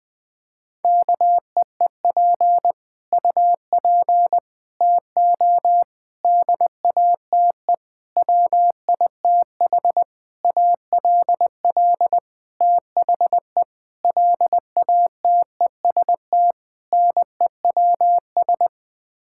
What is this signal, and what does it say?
Morse code, heard as a keyed tone, about 20 wpm, 700 Hz. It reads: KEEP UP TO DATE WITH ALL THE LATEST NEWS
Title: Sets of 9 Words - 20wpm